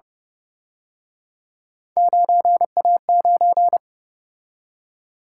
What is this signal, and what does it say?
Morse code, heard as a keyed tone, about 29 wpm, 700 Hz. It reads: 9A9